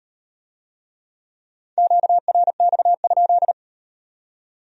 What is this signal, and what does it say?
Morse code, heard as a keyed tone, about 36 wpm, 700 Hz. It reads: QRX?